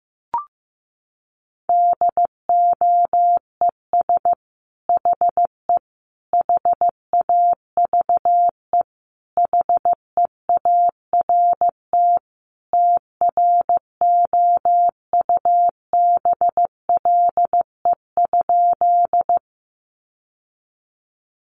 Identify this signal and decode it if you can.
Morse code, heard as a keyed tone, about 15 wpm, 700 Hz. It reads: DOES HE HAVE HEART TROUBLE?
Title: Sentences from Top 1200 Words 15wpm